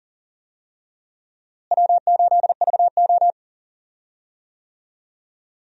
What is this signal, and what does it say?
Morse code, heard as a keyed tone, about 40 wpm, 700 Hz. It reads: W8VO
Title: USA Call Signs - 4 Characters - Repeated 2x - 40wpm